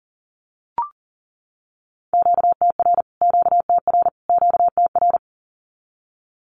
Morse code, heard as a keyed tone, about 40 wpm, 700 Hz. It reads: QTR QTR QTR
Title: Q-Codes Repeated 3x 40wpm